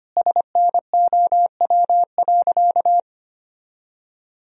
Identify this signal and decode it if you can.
Morse code, heard as a keyed tone, about 25 wpm, 700 Hz. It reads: SNOW.
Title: Sentences from Top 1000 Words - Review Part 3 - 25wpm